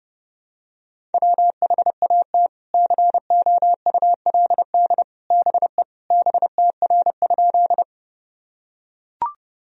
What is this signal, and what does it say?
Morse code, heard as a keyed tone, about 30 wpm, 700 Hz. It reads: WHAT COULD BE BTR?
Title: Sentences from Top 400 Words and Abbreviations 30wpm